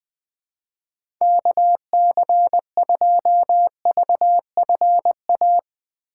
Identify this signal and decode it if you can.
Morse code, heard as a keyed tone, about 20 wpm, 700 Hz. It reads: KC2VFA
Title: USA Call Signs - 6 Characters - 3x Character Spacing - 20wpm